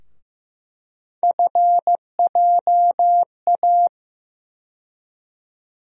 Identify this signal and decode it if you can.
Morse code, heard as a keyed tone, about 15 wpm, 700 Hz. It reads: FJA
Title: USA Call Sign Postfixes - 3 Letters 15wpm